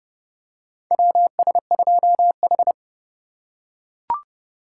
Morse code, heard as a keyed tone, about 30 wpm, 700 Hz. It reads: WS2H
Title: USA Call Signs - 4 Characters - 2x Character Spacing - 30wpm